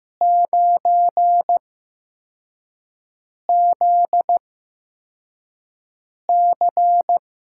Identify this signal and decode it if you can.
Morse code, heard as a keyed tone, about 16 wpm, spaced as for 4 wpm, 700 Hz. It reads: 9ZC
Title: USA Call Signs - 6 Characters - 8x Character Spacing - 15wpm